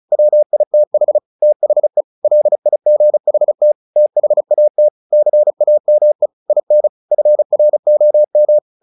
Morse code, heard as a keyed tone, about 34 wpm, 600 Hz. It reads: WITH THE LIGHT THAT CAME IN FROM
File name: the old man and the sea / the old man and the sea 1 35wpm